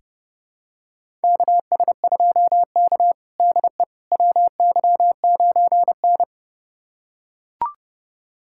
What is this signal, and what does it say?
Morse code, heard as a keyed tone, about 32 wpm, 700 Hz. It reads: KS2K DE WY9N
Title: USA Callsign DE Callsign - 4 Characters - 30wpm